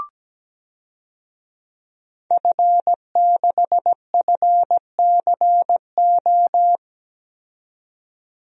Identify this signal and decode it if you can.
Morse code, heard as a keyed tone, about 17 wpm, 700 Hz. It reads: F6FCO
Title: DXCC-100 Call Signs - 5 Characters - 17wpm